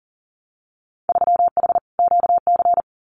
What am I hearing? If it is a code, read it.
Morse code, heard as a keyed tone, about 38 wpm, 700 Hz. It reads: 3H QC
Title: Winter Field Day Exchanges Encore 40wpm